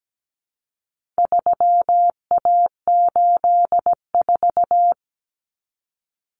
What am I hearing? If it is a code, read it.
Morse code, heard as a keyed tone, about 17 wpm, 700 Hz. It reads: 3A84
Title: Common Cut Numbers - 4-Digit 17wpm